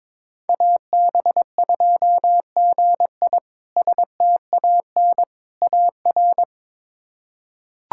Morse code, heard as a keyed tone, about 22 wpm, 700 Hz. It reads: AB2GI STAN AR
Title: K1USN SST QSO Practice - 22wpm